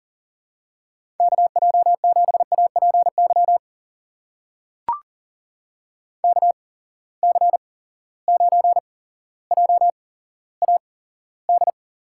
Morse code, heard as a keyed tone, about 40 wpm, spaced as for 17 wpm, 700 Hz. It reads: KJ7APY KC9JAD
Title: USA Call Signs - 6 Characters - 8x Character Spacing - 40wpm